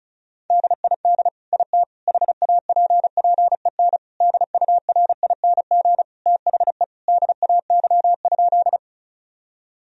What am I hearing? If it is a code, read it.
Morse code, heard as a keyed tone, about 36 wpm, 700 Hz. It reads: DID IT HAPPEN DURING THE DAY?
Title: Sentences from Top 1000 Words - Review Part 2 - 35wpm